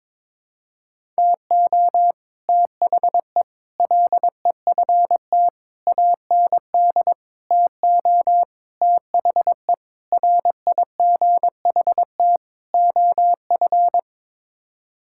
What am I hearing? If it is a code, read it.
Morse code, heard as a keyed tone, about 22 wpm, 700 Hz. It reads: TO THE LEFT AND TO THE RIGHT OF
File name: practice / 22wpm / Sets of 8 Words - 22wpm